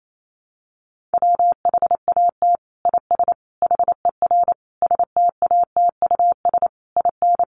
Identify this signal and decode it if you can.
Morse code, heard as a keyed tone, about 28 wpm, 700 Hz. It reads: WHAT IS HER STATUS IN
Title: Sentences from Top 1400 Words 28wpm